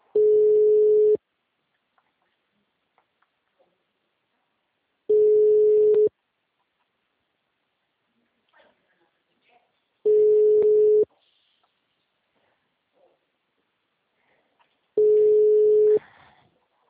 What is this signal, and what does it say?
Ringback tone (Europe, 425 Hz).
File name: out- -2004- - - .31068